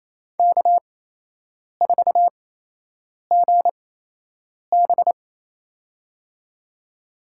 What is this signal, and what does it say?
Morse code, heard as a keyed tone, about 27 wpm, spaced as for 8 wpm, 700 Hz. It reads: K4GB
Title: USA Call Signs - 8x Character Spacing - 28wpm